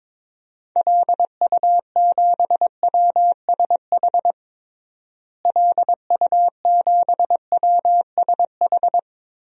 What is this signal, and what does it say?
Morse code, heard as a keyed tone, about 22 wpm, 700 Hz. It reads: LU7WSH LU7WSH
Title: DXCC-100 Call Signs - 6 Characters - Repeated 3x - 22wpm